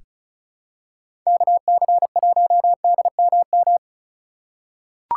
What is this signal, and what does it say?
Morse code, heard as a keyed tone, about 36 wpm, 700 Hz. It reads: KC1DMM
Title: USA Call Signs - 6 Characters - 3x Character Spacing - 35wpm